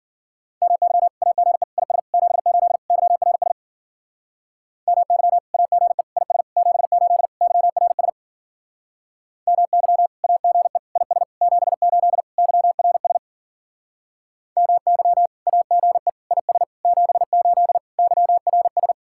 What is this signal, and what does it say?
Morse code, heard as a keyed tone, about 49 wpm, 700 Hz. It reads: MY AGE IS 78 YRS MY AGE IS 78 YRS MY AGE IS 78 YRS MY AGE IS 78 YRS
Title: Speed-Racing Common QSO Phrases Encore - 40wpm